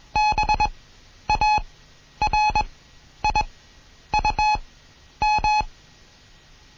Morse code, heard as a keyed tone, about 21 wpm, spaced as for 10 wpm, 850 Hz. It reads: BARIUM
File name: Morsen / Kurs A 20